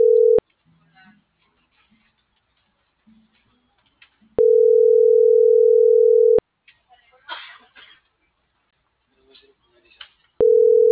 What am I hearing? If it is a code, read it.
Ringback tone (North America), 440 + 480 Hz.